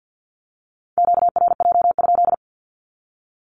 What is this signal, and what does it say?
Morse code, heard as a keyed tone, about 49 wpm, 700 Hz. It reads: QRJ?
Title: Q-Codes 50wpm